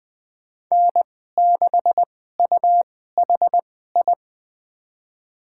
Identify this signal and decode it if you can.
Morse code, heard as a keyed tone, about 20 wpm, spaced as for 14 wpm, 700 Hz. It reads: N6UHI
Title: USA Call Signs - 5 Characters - 2x Character Spacing - 20wpm